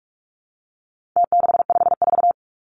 Morse code, heard as a keyed tone, about 44 wpm, 700 Hz. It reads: T654
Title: Numbers - Contest Style - 4-Digit 45wpm